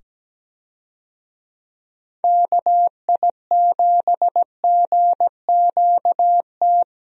Morse code, heard as a keyed tone, about 17 wpm, 700 Hz. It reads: KI7GQT
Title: USA Call Signs - 6 Characters - 17wpm